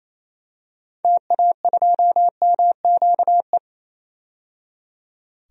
Morse code, heard as a keyed tone, about 28 wpm, 700 Hz. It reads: TA2MQE
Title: DXCC-100 Call Signs - 6 Characters - 28wpm